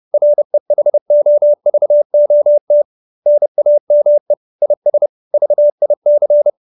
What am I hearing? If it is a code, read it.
Morse code, heard as a keyed tone, about 30 wpm, 590 Hz. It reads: REHOVOT NAME IS VIC
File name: QSO310 30